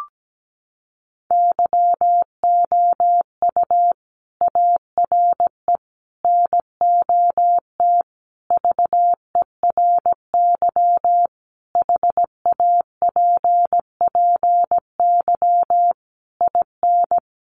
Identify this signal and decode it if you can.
Morse code, heard as a keyed tone, about 17 wpm, 700 Hz. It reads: YOU ARE NOT VERY HAPPY IN
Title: Sentences from Top 1200 Words 17wpm